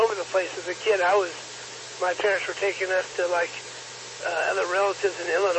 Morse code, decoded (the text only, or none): none